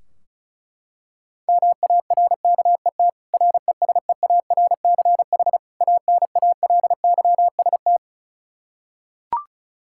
Morse code, heard as a keyed tone, about 36 wpm, 700 Hz. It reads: MARKET RESEARCH ANALYST
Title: Practice QSO - Exchange 3 - More Info - 35wpm